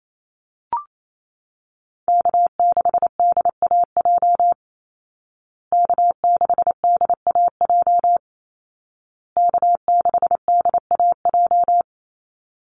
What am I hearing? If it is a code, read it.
Morse code, heard as a keyed tone, about 28 wpm, 700 Hz. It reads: K6DAJ K6DAJ K6DAJ
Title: Call Signs Repeated 2x - 28wpm